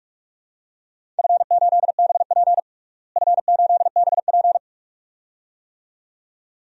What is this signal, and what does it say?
Morse code, heard as a keyed tone, about 46 wpm, 700 Hz. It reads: F8BP F8BP
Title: DXCC-100 Call Signs - 4 Characters - Repeated 2x - 45wpm